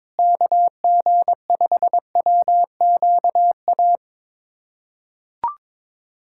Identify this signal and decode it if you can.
Morse code, heard as a keyed tone, about 21 wpm, 700 Hz. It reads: KG5WQA
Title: USA Callsign DE Callsign - 6 Characters - 22wpm